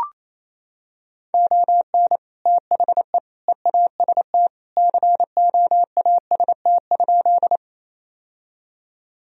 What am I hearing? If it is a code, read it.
Morse code, heard as a keyed tone, about 28 wpm, 700 Hz. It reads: ON THE EAST COAST?
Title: Sentences from Top 1000 Words - Review Part 4 - 28wpm